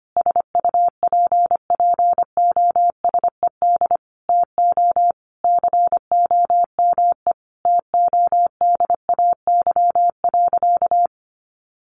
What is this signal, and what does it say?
Morse code, heard as a keyed tone, about 25 wpm, 700 Hz. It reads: SUPPOSED TO COME TODAY.
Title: Additional Easy Phrases 25wpm